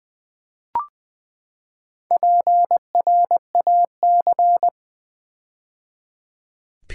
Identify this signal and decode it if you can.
Morse code, heard as a keyed tone, about 20 wpm, 700 Hz. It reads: PRAC
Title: Build A Word - 4 Letters At A Time - 12-Letter Words - 20wpm